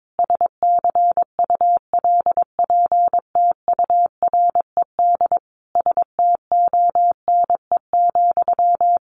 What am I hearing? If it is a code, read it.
Morse code, heard as a keyed tone, about 22 wpm, 700 Hz. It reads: SCULPTURED STONE,
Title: A Sound Of Thunder 22wpm